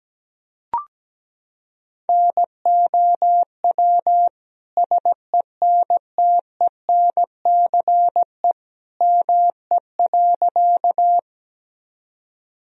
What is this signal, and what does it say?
Morse code, heard as a keyed tone, about 17 wpm, 700 Hz. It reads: NOW SENTENCE ME.
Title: Sentences from Top 1000 Words - Review Part 1 - 17wpm